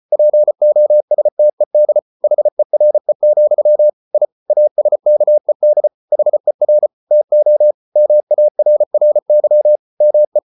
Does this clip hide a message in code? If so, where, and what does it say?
Yes: Morse code from 0.1 to 10.4 s, heard as a keyed tone, about 34 wpm, 600 Hz. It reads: POSTED HERE, I ASKED HER TO MARRY ME